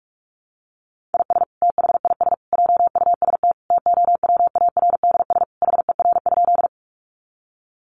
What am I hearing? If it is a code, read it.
Morse code, heard as a keyed tone, about 44 wpm, 700 Hz. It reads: IS THIS JUST TOWARDS HER?
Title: Sentences from Top 1100 Words 45wpm